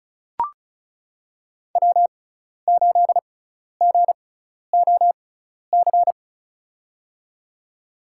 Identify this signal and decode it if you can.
Morse code, heard as a keyed tone, about 34 wpm, spaced as for 12 wpm, 700 Hz. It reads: W8GOC
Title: USA Call Signs - 6x Character Spacing - 35wpm